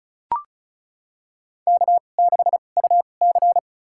Morse code, heard as a keyed tone, about 36 wpm, spaced as for 25 wpm, 700 Hz. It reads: K6UC
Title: USA Call Signs - 4 Characters - 2x Character Spacing - 35wpm